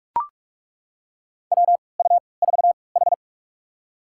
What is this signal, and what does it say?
Morse code, heard as a keyed tone, about 41 wpm, spaced as for 25 wpm, 700 Hz. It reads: WU4H
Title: USA Call Signs - 3x Character Spacing - 45wpm